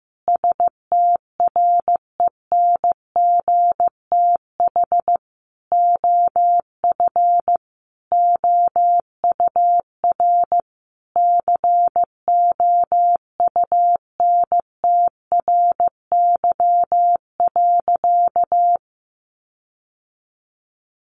Morse code, heard as a keyed tone, about 15 wpm, 700 Hz. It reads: STRENGTH OF OUR COUNTRY.
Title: Sentences from Top 1300 Words 15wpm